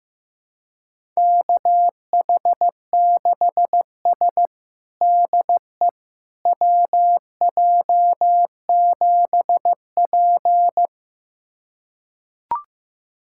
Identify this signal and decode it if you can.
Morse code, heard as a keyed tone, about 15 wpm, 700 Hz. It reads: KH6S DE WJ7P
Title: USA Callsign DE Callsign - 4 Characters - 15wpm